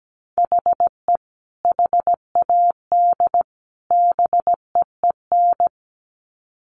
Morse code, heard as a keyed tone, about 17 wpm, 700 Hz. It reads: HE HAD BEEN
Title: Sets of 3 Words 17wpm